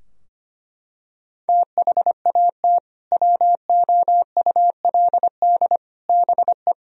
Morse code, heard as a keyed tone, about 25 wpm, 700 Hz. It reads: THAT WOULD BE
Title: Sentences from Top 1000 Words - Review Part 4 - 25wpm